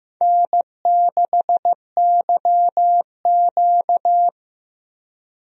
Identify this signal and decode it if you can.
Morse code, heard as a keyed tone, about 15 wpm, 700 Hz. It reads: N6YQ
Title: USA Call Signs - 4 Characters - Repeated 3x - 15wpm